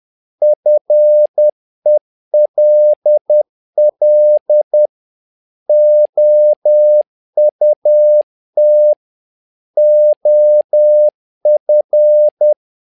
Morse code, heard as a keyed tone, about 10 wpm, 600 Hz. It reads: FELL OUT OF
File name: learning center / tower trouble / tower trouble 10wpm